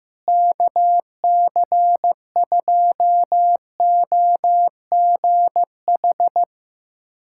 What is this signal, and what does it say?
Morse code, heard as a keyed tone, about 15 wpm, 700 Hz. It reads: KC2OGH
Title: USA Callsign DE Callsign 15wpm